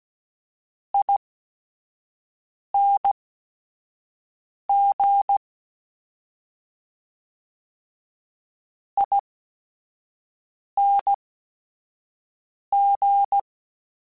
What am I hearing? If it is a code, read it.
Morse code, heard as a keyed tone, about 16 wpm, spaced as for 5 wpm, 800 Hz. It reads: ING ING
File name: Suffixes / General / ESE INGs